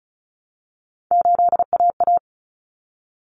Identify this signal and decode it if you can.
Morse code, heard as a keyed tone, about 34 wpm, 700 Hz. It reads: 8AA
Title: Common Cut Numbers - 3-Digit 35wpm